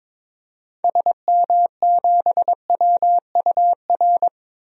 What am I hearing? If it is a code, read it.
Morse code, heard as a keyed tone, about 21 wpm, 700 Hz. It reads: SM7WUR
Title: DXCC-100 Call Signs - 6 Characters - 22wpm